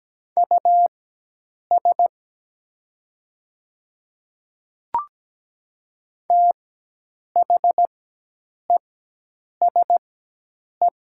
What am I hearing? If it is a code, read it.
Morse code, heard as a keyed tone, about 17 wpm, spaced as for 8 wpm, 700 Hz. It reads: US THESE
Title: 4x Character Spacing - Top 300 Words - 17wpm